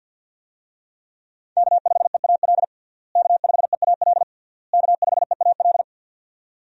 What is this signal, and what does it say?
Morse code, heard as a keyed tone, about 46 wpm, 700 Hz. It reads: K5EAL K5EAL K5EAL
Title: USA Call Signs - 5 Characters - Repeated 3x - 50wpm